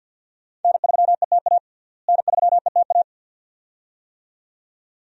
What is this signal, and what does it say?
Morse code, heard as a keyed tone, about 47 wpm, 700 Hz. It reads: N3ETA N3ETA
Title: USA Call Signs - 5 Characters - Repeated 2x - 50wpm